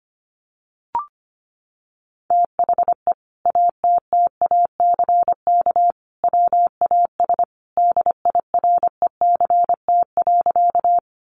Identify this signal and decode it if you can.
Morse code, heard as a keyed tone, about 25 wpm, 700 Hz. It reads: THE ATTACK WAS DIRECT.
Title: Sentences from Top 1500 Words 25wpm